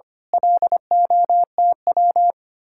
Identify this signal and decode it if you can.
Morse code, heard as a keyed tone, about 26 wpm, 700 Hz. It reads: LOTW
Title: Speed-Racing Common QSO Phrases Encore - 25wpm